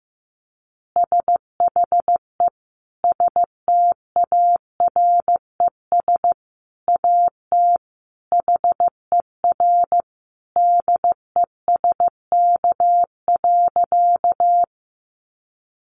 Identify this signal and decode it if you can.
Morse code, heard as a keyed tone, about 15 wpm, 700 Hz. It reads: SHE STARES AT HER DESK.
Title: Easy Phrases 15wpm